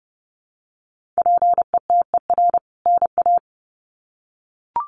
Morse code, heard as a keyed tone, about 30 wpm, 700 Hz. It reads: PETER NA
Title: CWT Exchange 30wpm